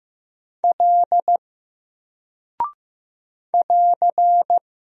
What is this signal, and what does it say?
Morse code, heard as a keyed tone, about 15 wpm, 700 Hz. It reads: L <AR>
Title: Single Character (Letter:Number:Punctuation:Prosign) 15wpm